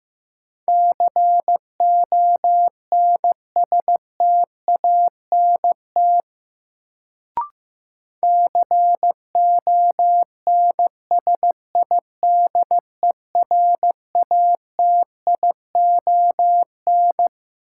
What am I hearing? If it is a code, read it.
Morse code, heard as a keyed tone, about 16 wpm, 700 Hz. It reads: CONSTANT CONSIDERATION
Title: Sentences from Top 1500 Words - New Word Drill Alt - 15wpm